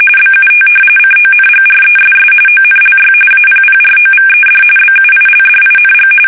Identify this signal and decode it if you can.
RTTY, about 45.45 baud. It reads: BUT CONGRESS ONLY APPROVED BUILDING
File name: fsk850